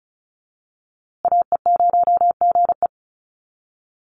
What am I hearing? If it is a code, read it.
Morse code, heard as a keyed tone, about 34 wpm, 700 Hz. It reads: AE0GE
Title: USA Call Signs Repeated 2x 35wpm